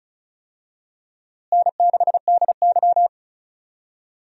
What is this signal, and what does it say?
Morse code, heard as a keyed tone, about 36 wpm, 700 Hz. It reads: N6DY
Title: USA Call Signs - 4 Characters - Repeated 3x - 35wpm